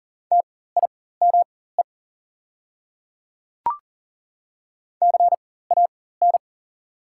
Morse code, heard as a keyed tone, about 41 wpm, spaced as for 19 wpm, 690 Hz. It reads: TIME CAN
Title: 4x Character Spacing - Top 100 Words - 40wpm